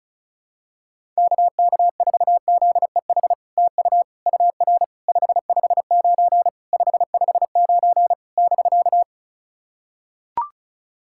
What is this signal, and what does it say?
Morse code, heard as a keyed tone, about 34 wpm, 700 Hz. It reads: KK4ZEH TU UR 559 559 <BK>
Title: SOTA QSO Practice 35wpm